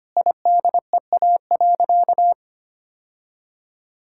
Morse code, heard as a keyed tone, about 25 wpm, 700 Hz. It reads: IDEA.
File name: practice / 25wpm / Sentences with 3-Word Phrases - 25wpm